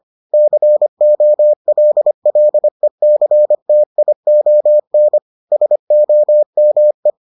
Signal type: Morse code

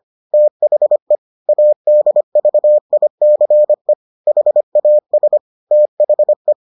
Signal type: Morse code

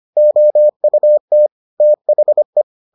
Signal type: Morse code

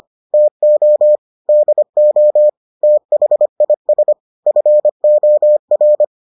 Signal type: Morse code

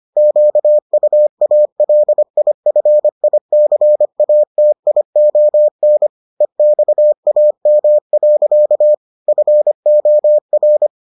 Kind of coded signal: Morse code

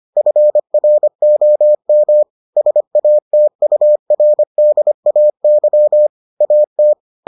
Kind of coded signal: Morse code

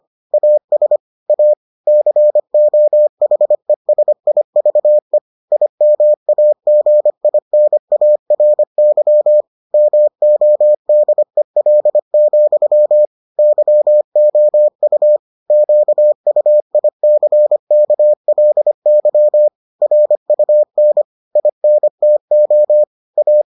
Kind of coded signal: Morse code

